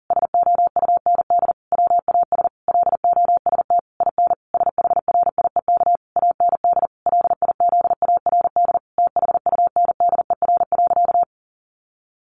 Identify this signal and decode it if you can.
Morse code, heard as a keyed tone, about 40 wpm, 700 Hz. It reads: SOUND WAS LOST IN SHRIEK AND LIZARD THUNDER.
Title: A Sound Of Thunder 40wpm